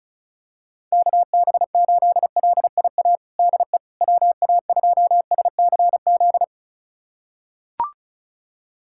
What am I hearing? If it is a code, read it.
Morse code, heard as a keyed tone, about 34 wpm, 690 Hz. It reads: KB8LIA DE WA2SCZ
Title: USA Callsign DE Callsign - 6 Characters - 35wpm